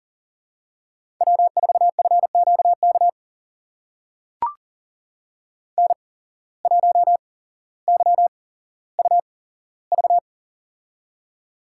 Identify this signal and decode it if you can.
Morse code, heard as a keyed tone, about 43 wpm, spaced as for 17 wpm, 700 Hz. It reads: W4FQK N1YUV
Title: USA Call Signs - 5 Characters - 8x Character Spacing - 40wpm